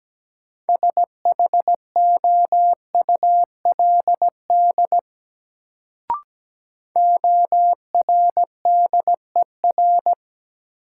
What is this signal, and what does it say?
Morse code, heard as a keyed tone, about 17 wpm, 700 Hz. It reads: SHOULD ORDER
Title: Sets of 2 Words - New Word Drill - 17wpm